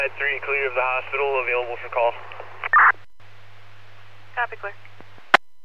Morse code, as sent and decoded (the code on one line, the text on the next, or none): none